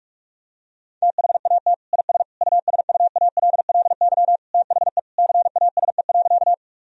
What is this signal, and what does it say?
Morse code, heard as a keyed tone, about 46 wpm, 700 Hz. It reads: THAT IS USUALLY THE CASE.
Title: Sentences from Top 1000 Words - Review Part 3 - 45wpm